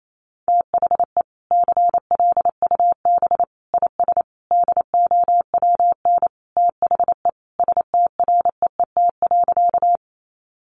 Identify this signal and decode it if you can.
Morse code, heard as a keyed tone, about 28 wpm, 700 Hz. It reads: THE CLUB IS DOWN THE STREET.
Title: Sentences from Top 1200 Words 28wpm